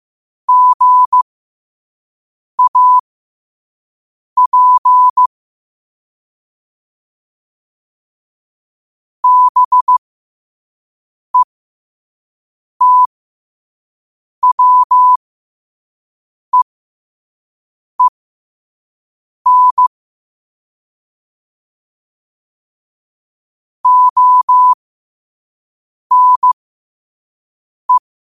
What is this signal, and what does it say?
Morse code, heard as a keyed tone, about 15 wpm, spaced as for 5 wpm, 1000 Hz. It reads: GAP BETWEEN ONE
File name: qotdfiles / 5wpm